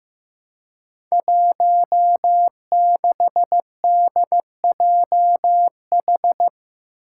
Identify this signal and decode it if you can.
Morse code, heard as a keyed tone, about 15 wpm, 700 Hz. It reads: 16DJH
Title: Five Challenging Letter-Number Combinations 15wpm